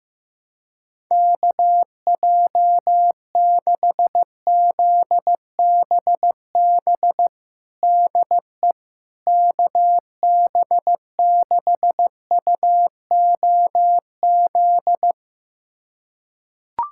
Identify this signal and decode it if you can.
Morse code, heard as a keyed tone, about 16 wpm, 700 Hz. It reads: KJ6ZBB DE KB6UOZ
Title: USA Callsign DE Callsign 15wpm